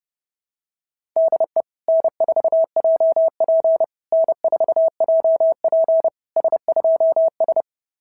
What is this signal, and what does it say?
Morse code, heard as a keyed tone, about 30 wpm, 690 Hz. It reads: DE N4JP N4JP S2S
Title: SOTA QSO Practice 30wpm